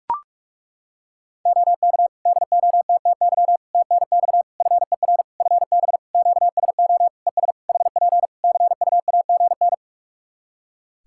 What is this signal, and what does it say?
Morse code, heard as a keyed tone, about 41 wpm, 700 Hz. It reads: OK DOTTY TNX FER FB QSO ES HP CUAGN
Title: K1USN SST QSO Practice 45wpm